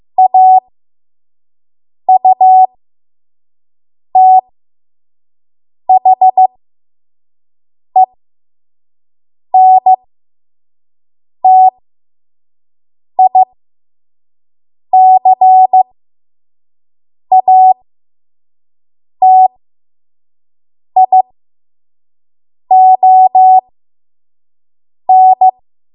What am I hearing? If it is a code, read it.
Morse code, heard as a keyed tone, about 15 wpm, spaced as for 5 wpm, 750 Hz. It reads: AUTHENTICATION
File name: Morse / 05WPM